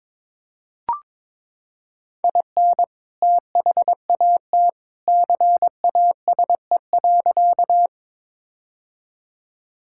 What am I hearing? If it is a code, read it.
Morse code, heard as a keyed tone, about 22 wpm, 700 Hz. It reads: IN THAT CASE.
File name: practice / 22wpm / Sentences with 2-Word Phrases - 22wpm